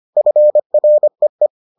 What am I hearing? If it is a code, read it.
Morse code, heard as a keyed tone, about 25 wpm, 600 Hz. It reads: FREE